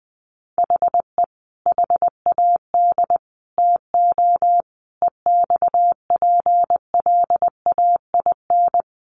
Morse code, heard as a keyed tone, about 20 wpm, 700 Hz. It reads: HE HAD TO EXPLAIN
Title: Sentences from Top 1100 Words 20wpm